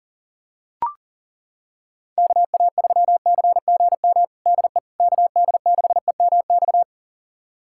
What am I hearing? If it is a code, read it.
Morse code, heard as a keyed tone, about 40 wpm, 700 Hz. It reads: KA3CGM DE KD6EMX
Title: USA Callsign DE Callsign 40wpm